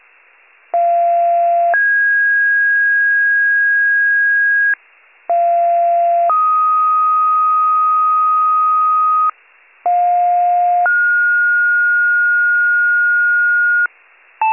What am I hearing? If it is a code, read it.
Two-tone page, heard as a one-second tone, then a three-second tone, about 690 then 1650 Hz, about 690 then 1250 Hz, about 710 then 1430 Hz. It is heard from 0.7 to 13.9 s.